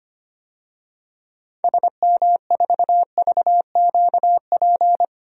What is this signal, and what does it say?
Morse code, heard as a keyed tone, about 25 wpm, 700 Hz. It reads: SM4VQP
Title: DXCC-100 Call Signs - 6 Characters - 25wpm